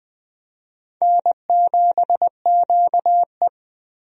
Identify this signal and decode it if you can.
Morse code, heard as a keyed tone, about 20 wpm, 700 Hz. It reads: N7QE
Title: USA Call Signs - 4 Characters - 20wpm